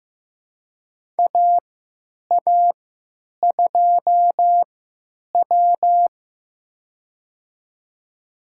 Morse code, heard as a keyed tone, about 15 wpm, spaced as for 9 wpm, 700 Hz. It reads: AA2W
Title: USA Call Signs - 4 Characters - 3x Character Spacing - 15wpm